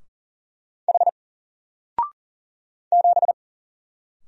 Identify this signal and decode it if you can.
Morse code, heard as a keyed tone, about 40 wpm, 700 Hz. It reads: H 7
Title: Single Letter-Number 40wpm